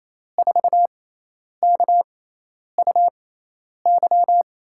Morse code, heard as a keyed tone, about 28 wpm, spaced as for 10 wpm, 700 Hz. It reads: 4KUY
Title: USA Call Signs - 6 Characters - 6x Character Spacing - 28wpm